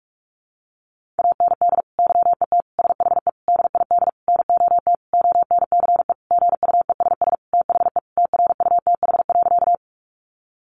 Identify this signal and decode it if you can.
Morse code, heard as a keyed tone, about 44 wpm, 700 Hz. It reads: AND YET SHE DID NOT ONCE GUESS THE TRUTH.
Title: Sentences from Top 1500 Words 45wpm